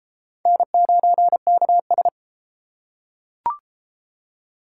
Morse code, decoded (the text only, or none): N9KS